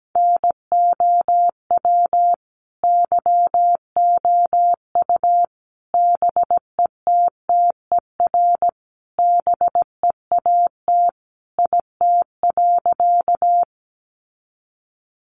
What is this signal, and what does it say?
Morse code, heard as a keyed tone, about 17 wpm, 700 Hz. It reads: NOW YOU BETTER BEAT IT.
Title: Sentences from Top 1300 Words 17wpm